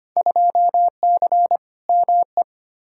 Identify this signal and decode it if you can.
Morse code, heard as a keyed tone, about 25 wpm, 700 Hz. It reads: 2C ME
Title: Field Day Exchanges Encore - 25wpm